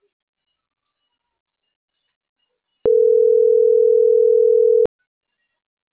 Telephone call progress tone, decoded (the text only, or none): ringback tone